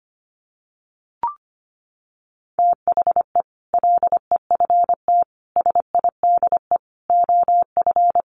Morse code, heard as a keyed tone, about 25 wpm, 700 Hz. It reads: THE LEFT SIDE OF
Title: Sets of 5 Words Encore 25wpm